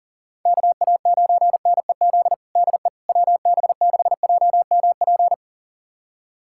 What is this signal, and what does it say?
Morse code, heard as a keyed tone, about 40 wpm, 700 Hz. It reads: KA9NEZ DE WB6JMP
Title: USA Callsign DE Callsign 40wpm